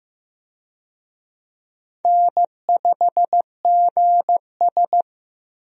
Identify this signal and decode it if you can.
Morse code, heard as a keyed tone, about 15 wpm, 700 Hz. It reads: N5GS
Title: USA Call Signs - 4 Characters - Repeated 2x - 15wpm